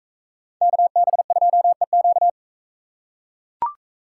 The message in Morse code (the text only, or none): KB2EQ